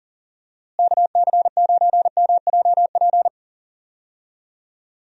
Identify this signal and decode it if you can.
Morse code, heard as a keyed tone, about 38 wpm, 700 Hz. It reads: KC9MJP
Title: USA Call Signs - 6 Characters - Repeated 2x - 40wpm